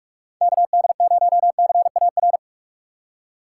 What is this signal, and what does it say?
Morse code, heard as a keyed tone, about 44 wpm, 700 Hz. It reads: K D 0 C A R
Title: USA Call Signs - 6 Characters - 6x Character Spacing - 45wpm